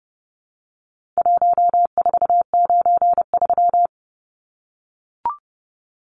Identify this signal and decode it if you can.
Morse code, heard as a keyed tone, about 30 wpm, 700 Hz. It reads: 1493
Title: Numbers - Contest Style - 4-Digit 30wpm